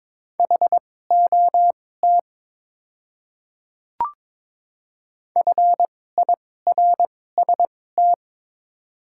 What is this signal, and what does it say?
Morse code, heard as a keyed tone, about 21 wpm, spaced as for 16 wpm, 700 Hz. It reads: HOT FIRST
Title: 2x Character Spacing - Top 100 Words - 22wpm